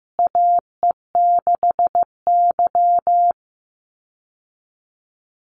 Morse code, heard as a keyed tone, about 15 wpm, 700 Hz. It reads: AE6Y
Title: Call Signs Repeated 3x - 15wpm